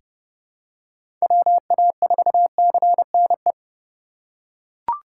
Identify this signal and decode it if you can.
Morse code, heard as a keyed tone, about 30 wpm, 700 Hz. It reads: WA4CNE
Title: USA Call Signs - 6 Characters - 4x Character Spacing - 30wpm